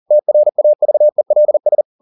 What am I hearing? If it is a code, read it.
Morse code, heard as a keyed tone, about 40 wpm, 600 Hz. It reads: TRAVELS